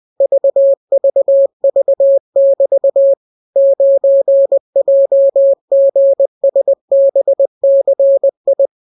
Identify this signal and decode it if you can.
Morse code, heard as a keyed tone, about 20 wpm, 550 Hz. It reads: VVV<BT> 9JGSBCI